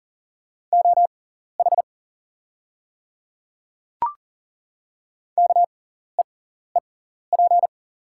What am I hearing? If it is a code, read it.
Morse code, heard as a keyed tone, about 40 wpm, spaced as for 14 wpm, 700 Hz. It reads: OH KEEP
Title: 6x Character Spacing - Top 500 Words - 40wpm